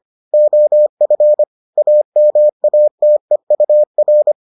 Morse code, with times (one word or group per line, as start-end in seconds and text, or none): 0.3-1.4 OF
1.8-4.3 AMATEUR